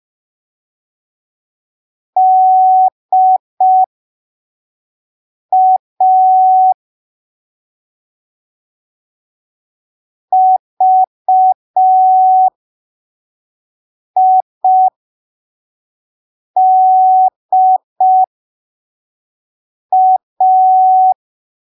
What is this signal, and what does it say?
Morse code, heard as a keyed tone, about 5 wpm, spaced as for 3 wpm, 750 Hz. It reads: DA VIDA
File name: arquivos / texto / texto 25 05ppm